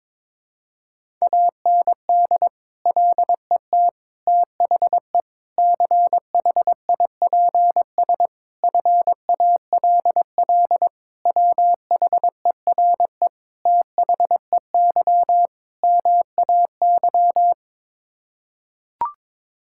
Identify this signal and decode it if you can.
Morse code, heard as a keyed tone, about 22 wpm, 700 Hz. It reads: AND LET THE CHIPS FALL WHERE THEY MAY
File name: practice / 22wpm / Sets of 8 Words - 22wpm